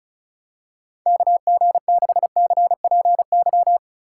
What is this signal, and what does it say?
Morse code, heard as a keyed tone, about 34 wpm, 700 Hz. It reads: KG6CPY
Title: USA Call Signs - 6 Characters - 4x Character Spacing - 35wpm